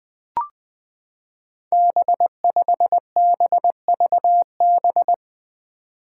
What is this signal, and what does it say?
Morse code, heard as a keyed tone, about 20 wpm, 700 Hz. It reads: B5BVB
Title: Five Challenging Letter-Number Combinations 20wpm